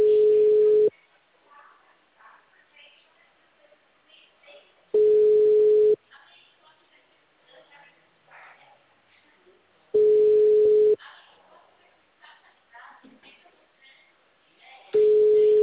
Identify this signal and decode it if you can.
Ringback tone (Europe, 425 Hz), with speech.
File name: out- -2004- - - .74823